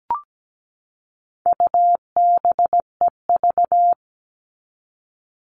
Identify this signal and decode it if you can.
Morse code, heard as a keyed tone, about 17 wpm, 700 Hz. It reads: UBEV
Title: All Cut Numbers - 4-Digit 17wpm